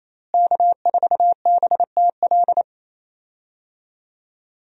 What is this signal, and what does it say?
Morse code, heard as a keyed tone, about 27 wpm, 700 Hz. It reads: K4BTL
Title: USA Call Signs - 5 Characters - Repeated 3x - 28wpm